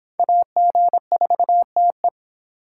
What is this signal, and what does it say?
Morse code, heard as a keyed tone, about 26 wpm, 700 Hz. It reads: AG4TE